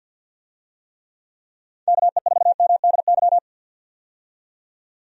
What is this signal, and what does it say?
Morse code, heard as a keyed tone, about 47 wpm, 700 Hz. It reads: KE4MDY
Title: USA Call Signs - 6 Characters - 50wpm